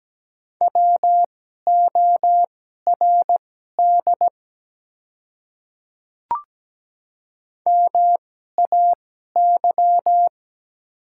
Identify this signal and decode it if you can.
Morse code, heard as a keyed tone, about 17 wpm, spaced as for 12 wpm, 700 Hz. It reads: WORD MAY
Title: 2x Character Spacing - Top 200 Words - 17wpm